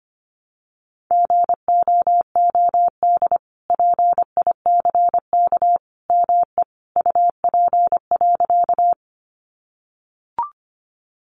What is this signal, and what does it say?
Morse code, heard as a keyed tone, about 25 wpm, 700 Hz. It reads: GOOD PICK ME UP.
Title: Sentences from Top 1000 Words 25wpm